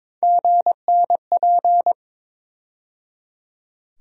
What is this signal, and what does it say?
Morse code, heard as a keyed tone, about 21 wpm, 700 Hz. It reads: GNP
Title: USA Call Sign Postfixes - 3 Letters 22wpm